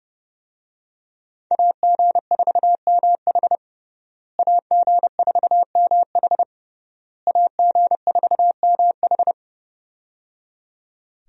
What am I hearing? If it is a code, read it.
Morse code, heard as a keyed tone, about 30 wpm, 700 Hz. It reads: AG4MH AG4MH AG4MH
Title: USA Call Signs - 5 Characters - Repeated 3x - 30wpm